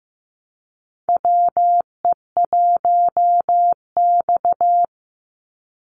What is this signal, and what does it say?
Morse code, heard as a keyed tone, about 15 wpm, 700 Hz. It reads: WE1X
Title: USA Call Signs 15wpm